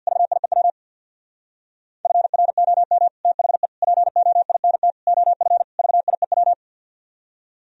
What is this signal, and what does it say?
Morse code, heard as a keyed tone, about 64 wpm, 700 Hz. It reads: VIEW FROM THE POINT OF VIEW